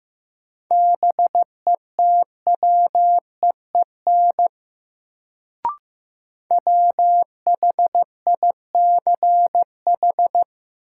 Morse code, heard as a keyed tone, about 15 wpm, 700 Hz. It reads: BETWEEN WHICH
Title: Sets of 2 Words - New Word Drill - 15wpm